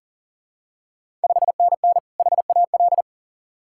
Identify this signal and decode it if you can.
Morse code, heard as a keyed tone, about 39 wpm, 690 Hz. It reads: 5NN HAL
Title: Ontario QSO Party Exchange 40wpm